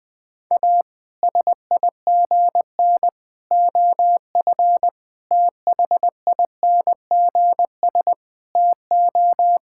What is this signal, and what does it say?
Morse code, heard as a keyed tone, about 20 wpm, 700 Hz. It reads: A SIGN OF THINGS TO
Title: Sets of 6 Words - 20wpm